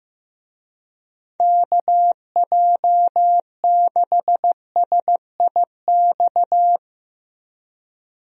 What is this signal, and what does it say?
Morse code, heard as a keyed tone, about 15 wpm, 700 Hz. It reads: KJ6SIX
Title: USA Call Signs - 6 Characters - Repeated 2x - 15wpm